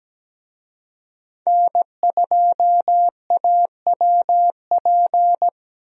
Morse code, heard as a keyed tone, about 17 wpm, 700 Hz. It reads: N2AWP
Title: USA Call Signs - 5 Characters - 17wpm